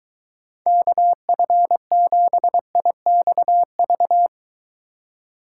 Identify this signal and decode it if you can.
Morse code, heard as a keyed tone, about 23 wpm, 700 Hz. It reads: KF7IXV